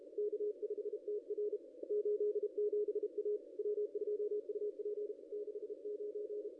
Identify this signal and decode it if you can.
Morse code, heard as a keyed tone, about 35 wpm, 410 Hz. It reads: K5TR 87 A W2UP 69
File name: contests / ss-cw-k5tr-op-n5rz -2200